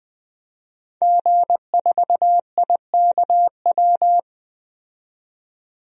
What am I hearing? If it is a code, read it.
Morse code, heard as a keyed tone, about 21 wpm, 700 Hz. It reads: G4IKW
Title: DXCC-100 Call Signs - 5 Characters - 20wpm